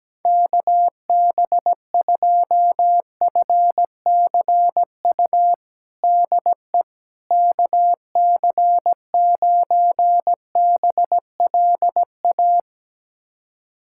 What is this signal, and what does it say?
Morse code, heard as a keyed tone, about 17 wpm, 700 Hz. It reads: KB2FCU DE KC9BLA
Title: USA Callsign DE Callsign 17wpm